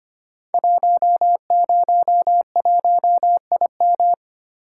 Morse code, heard as a keyed tone, about 25 wpm, 700 Hz. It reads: 101IM
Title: Code Groups - 5 Letter:Numbers - 25wpm